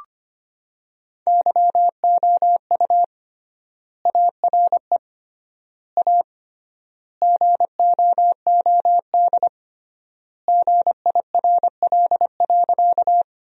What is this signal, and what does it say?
Morse code, heard as a keyed tone, about 25 wpm, 700 Hz. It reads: YOU ARE A GOOD GIRL.
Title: 3x Word-Spacing - Sentences from Top 400 Words - 25wpm